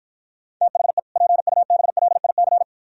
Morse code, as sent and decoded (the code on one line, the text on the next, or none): - .... / .--. ..- -... .-.. .. -.-.
THE PUBLIC